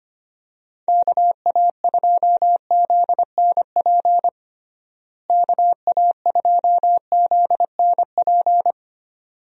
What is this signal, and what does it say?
Morse code, heard as a keyed tone, about 25 wpm, 700 Hz. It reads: KA2ZNP KA2ZNP
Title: USA Call Signs - 6 Characters - Repeated 2x - 25wpm